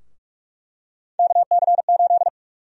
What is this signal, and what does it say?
Morse code, heard as a keyed tone, about 46 wpm, 700 Hz. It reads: KC8
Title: USA Call Sign Prefixes - Repeated 2x - 45wpm